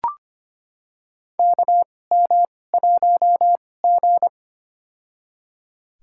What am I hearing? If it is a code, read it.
Morse code, heard as a keyed tone, about 25 wpm, spaced as for 18 wpm, 700 Hz. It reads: KM1G
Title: USA Call Signs - 4 Characters - 2x Character Spacing - 25wpm